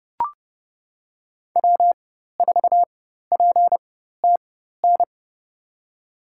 Morse code, heard as a keyed tone, about 32 wpm, spaced as for 14 wpm, 700 Hz. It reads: W4PTN
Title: USA Call Signs - 5 Characters - 4x Character Spacing - 30wpm